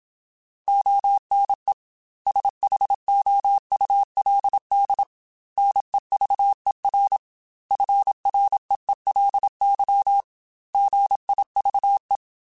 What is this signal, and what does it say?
Morse code, heard as a keyed tone, about 26 wpm, 800 Hz. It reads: ONE SHOULD NEVER FREELY GIVE
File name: Quotes / Quotes-mp3s / Advice @25